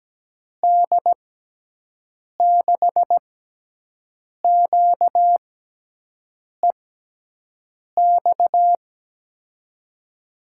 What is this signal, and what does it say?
Morse code, heard as a keyed tone, about 17 wpm, spaced as for 6 wpm, 700 Hz. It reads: D6QEX
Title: USA Call Signs - 6x Character Spacing - 17wpm